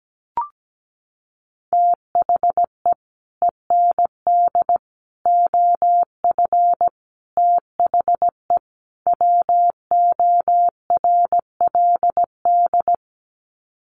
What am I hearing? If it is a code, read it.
Morse code, heard as a keyed tone, about 17 wpm, 700 Hz. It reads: THE END OF THE WORLD
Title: Sets of 5 Words 17wpm